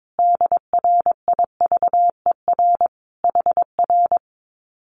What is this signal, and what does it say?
Morse code, heard as a keyed tone, about 22 wpm, 700 Hz. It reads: DRIVER HR
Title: Common QSO Phrases Encore 22wpm